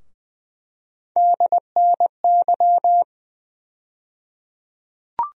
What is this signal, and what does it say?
Morse code, heard as a keyed tone, about 21 wpm, 700 Hz. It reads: DNY
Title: USA Call Sign Postfixes - 3 Letters 20wpm